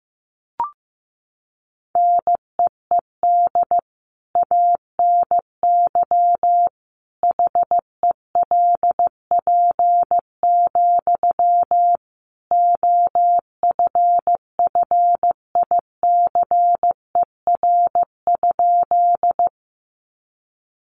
Morse code, heard as a keyed tone, about 15 wpm, 700 Hz. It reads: NEED ANY HELP, OFFICER?
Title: Sentences from Top 1200 Words 15wpm